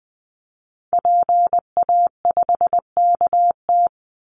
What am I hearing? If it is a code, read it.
Morse code, heard as a keyed tone, about 20 wpm, 700 Hz. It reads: PA5KT
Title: Call Signs Repeated 3x - 20wpm